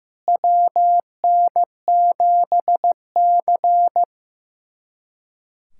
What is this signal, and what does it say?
Morse code, heard as a keyed tone, about 16 wpm, 700 Hz. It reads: WN7C
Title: USA Callsign DE Callsign - 4 Characters - 15wpm